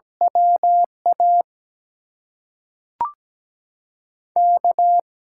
Morse code, heard as a keyed tone, about 17 wpm, spaced as for 5 wpm, 700 Hz. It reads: WAK
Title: USA Call Signs - 6 Characters - 8x Character Spacing - 17wpm